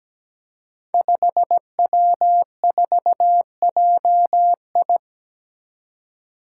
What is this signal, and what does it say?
Morse code, heard as a keyed tone, about 17 wpm, 700 Hz. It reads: 5W4JI
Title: Five Challenging Letter-Number Combinations 17wpm